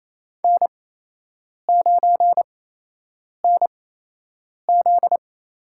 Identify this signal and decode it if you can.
Morse code, heard as a keyed tone, about 28 wpm, spaced as for 8 wpm, 700 Hz. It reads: N9NZ
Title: USA Call Signs - 8x Character Spacing - 28wpm